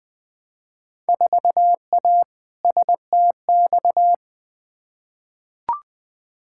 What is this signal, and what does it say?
Morse code, heard as a keyed tone, about 20 wpm, 700 Hz. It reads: 4A STX
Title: Field Day Exchanges - 20wpm